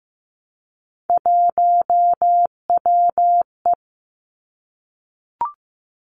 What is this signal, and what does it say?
Morse code, heard as a keyed tone, about 15 wpm, 700 Hz. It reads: 1WE
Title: Three Challenging Letter-Number Combinations 15wpm